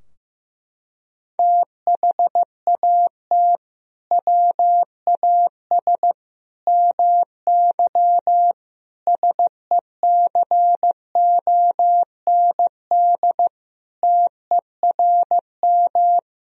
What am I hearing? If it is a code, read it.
Morse code, heard as a keyed tone, about 15 wpm, 700 Hz. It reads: THAT WAS MY SECOND TERM
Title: Sentences from Top 1000 Words - Review Part 4 - 15wpm